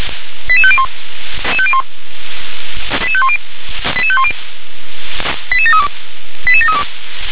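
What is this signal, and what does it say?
Five-tone selective call (ZVEI), heard as quick runs of five tones, tones of about 70 ms. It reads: 88501 88510 88510 88732 88732
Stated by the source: ZVEI five-tone selective call received on an SDR.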